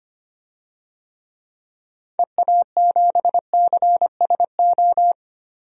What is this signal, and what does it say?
Morse code, heard as a keyed tone, about 25 wpm, 700 Hz. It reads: EA7CSO
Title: DXCC-100 Call Signs - 6 Characters - Repeated 3x - 25wpm